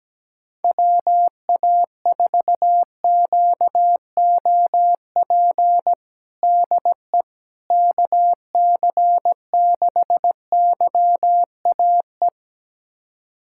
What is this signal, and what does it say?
Morse code, heard as a keyed tone, about 17 wpm, 700 Hz. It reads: WA4QOP DE KC6YAE
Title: USA Callsign DE Callsign 17wpm